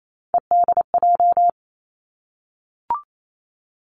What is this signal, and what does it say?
Morse code, heard as a keyed tone, about 28 wpm, 700 Hz. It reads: EDJ
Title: Three Challenging Letter-Number Combinations 28wpm